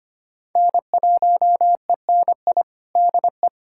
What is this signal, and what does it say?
Morse code, heard as a keyed tone, about 25 wpm, 700 Hz. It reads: N1ENI DE
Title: USA Callsign DE Callsign 25wpm